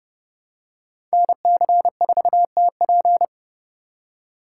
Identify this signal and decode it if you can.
Morse code, heard as a keyed tone, about 29 wpm, 700 Hz. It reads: NC4TP